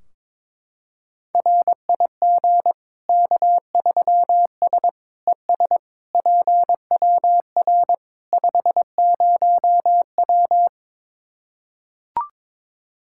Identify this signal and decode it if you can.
Morse code, heard as a keyed tone, about 22 wpm, 700 Hz. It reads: RIG K3S ES PWR 50W
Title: K1USN SST QSO Practice 22wpm